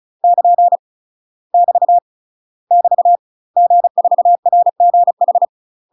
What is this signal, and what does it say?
Morse code, heard as a keyed tone, about 34 wpm, 700 Hz. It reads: <KN> <BT> <BT> G4RGH
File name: UK MASTER QSOs TRAINING / 35WPM / UK MASTER QSOs 0019